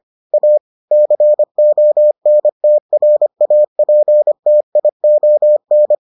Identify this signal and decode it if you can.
Morse code, heard as a keyed tone, about 25 wpm, 600 Hz. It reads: A CONTRAPTION